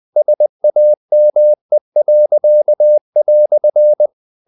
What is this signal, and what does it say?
Morse code, heard as a keyed tone, about 20 wpm, 600 Hz. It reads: SAME."
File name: CW / Practice / EnglishQuotations0320